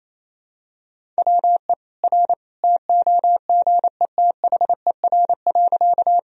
Morse code, heard as a keyed tone, about 28 wpm, 700 Hz. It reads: WE R TOGETHER.
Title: Sentences from Top 300 Words and Abbreviations 28wpm